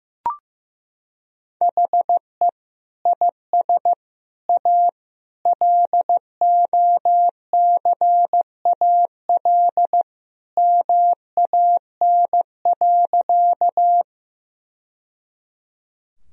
Morse code, heard as a keyed tone, about 15 wpm, 700 Hz. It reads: HE IS A LOCAL MAN.
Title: Sentences from Top 1000 Words - Review Part 3 - 15wpm